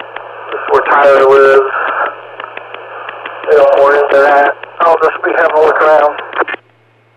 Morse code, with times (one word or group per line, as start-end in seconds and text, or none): none